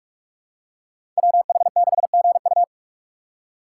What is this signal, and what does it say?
Morse code, heard as a keyed tone, about 46 wpm, 700 Hz. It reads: WH6GU
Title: USA Call Signs - 5 Characters - 6x Character Spacing - 45wpm